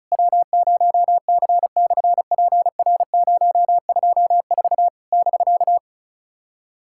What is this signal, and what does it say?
Morse code, heard as a keyed tone, about 36 wpm, 700 Hz. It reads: W0C/PR024 <BK>
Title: SOTA QSO Practice 35wpm